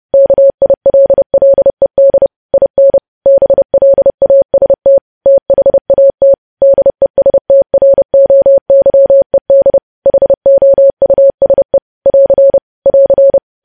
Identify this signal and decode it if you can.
Morse code, heard as a keyed tone, about 30 wpm, 550 Hz. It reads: KILLED IN BLAST THAT DESTROYED HOUSE <AR> <AR>
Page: news headlines in morse code at 30 wpm